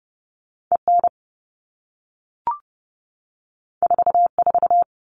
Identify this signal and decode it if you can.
Morse code, heard as a keyed tone, about 30 wpm, 700 Hz. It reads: EN 44
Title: All Cut Numbers - 2-Digit 30wpm